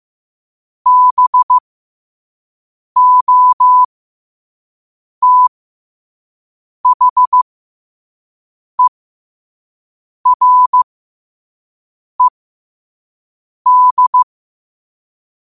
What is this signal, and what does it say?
Morse code, heard as a keyed tone, about 15 wpm, spaced as for 5 wpm, 1000 Hz. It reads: BOTHERED